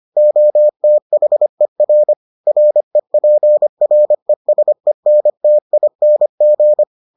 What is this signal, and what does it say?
Morse code, heard as a keyed tone, about 25 wpm, 600 Hz. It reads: OTHER REPRESENTING